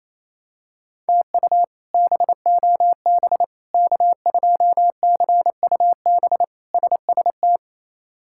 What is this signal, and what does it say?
Morse code, heard as a keyed tone, about 28 wpm, 700 Hz. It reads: TU BOB K2CUB SST
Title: K1USN SST QSO Practice - 28wpm